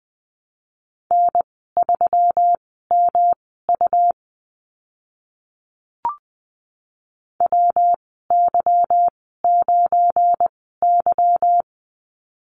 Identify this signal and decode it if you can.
Morse code, heard as a keyed tone, about 20 wpm, spaced as for 14 wpm, 700 Hz. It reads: N3MU WY9Y
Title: 186 Lesson 30g - Letter b - Call Signs 20wpm